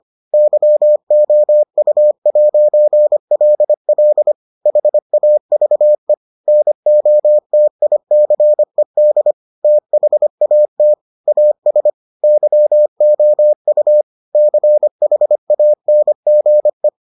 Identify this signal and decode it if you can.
Morse code, heard as a keyed tone, about 25 wpm, 600 Hz. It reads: YOU'LL HAVE NOTICED THAT AS YOU CHANGE